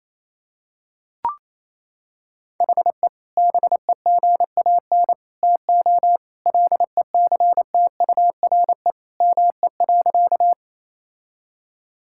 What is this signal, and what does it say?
Morse code, heard as a keyed tone, about 27 wpm, 700 Hz. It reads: HE BEGAN TO LECTURE ME.
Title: Sentences from Top 1600 Words 28wpm